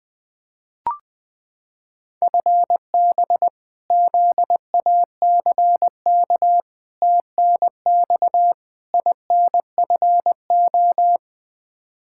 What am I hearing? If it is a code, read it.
Morse code, heard as a keyed tone, about 20 wpm, 700 Hz. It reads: FB ZACK TNX INFO
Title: Practice QSO - Exchange 3 - More Info - 20wpm